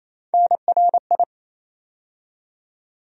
Morse code, heard as a keyed tone, about 28 wpm, 700 Hz. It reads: NRI
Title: USA Call Sign Postfixes - 3 Letters 28wpm